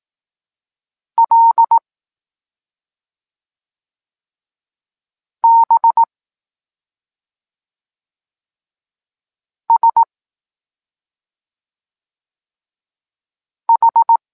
Morse code, heard as a keyed tone, about 18 wpm, 920 Hz. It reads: L B S H
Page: Tekst 23.3 – Ciąg dalszy doskonalenia odbioru poznanych liter